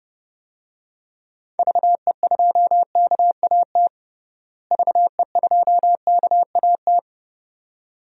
Morse code, heard as a keyed tone, about 32 wpm, 700 Hz. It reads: VE2KAT VE2KAT
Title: DXCC-100 Call Signs - 6 Characters - Repeated 2x - 30wpm